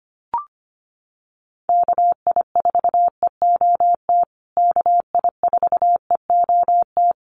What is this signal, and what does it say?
Morse code, heard as a keyed tone, about 25 wpm, 700 Hz. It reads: KI4EOT KI4EOT
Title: USA Call Signs Repeated 3x 25wpm